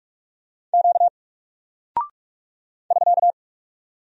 Morse code, heard as a keyed tone, about 46 wpm, 700 Hz. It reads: Q <SK>
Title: Single Character (Letter:Number:Punctuation:Prosign) Alt 45wpm